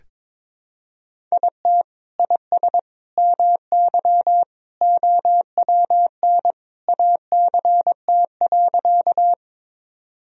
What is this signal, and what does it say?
Morse code, heard as a keyed tone, about 21 wpm, 700 Hz. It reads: IT IS MY OWN ACT.
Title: Sentences from Top 1000 Words - Review Part 1 - 22wpm